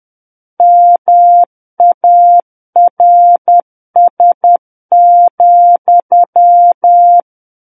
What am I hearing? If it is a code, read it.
Morse code, heard as a keyed tone, about 10 wpm, 700 Hz. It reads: MARS,
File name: Book One / 01-Chapter 1 - 10 WPM